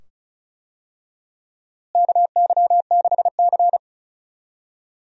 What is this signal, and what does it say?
Morse code, heard as a keyed tone, about 36 wpm, 700 Hz. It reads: KY6C
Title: USA Call Signs - 4 Characters - Repeated 2x - 35wpm